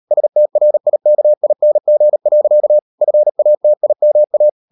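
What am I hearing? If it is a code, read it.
Morse code, heard as a keyed tone, about 38 wpm, 600 Hz. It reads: STRIKING. FATIMA
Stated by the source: Morse code practice at 38 WPM, sending Ken Follett's novel Never in English, from CW med Gnister.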